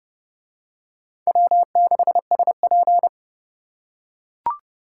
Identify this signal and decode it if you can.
Morse code, heard as a keyed tone, about 30 wpm, 700 Hz. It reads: W6SP